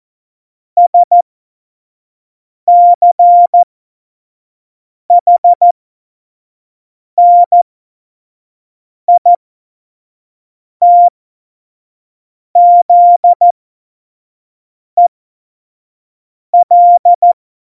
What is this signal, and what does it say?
Morse code, heard as a keyed tone, about 14 wpm, spaced as for 5 wpm, 700 Hz. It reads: SCHNITZEL
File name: Morsekurs / Witze / Witz 11. Februar 2020, 25 F70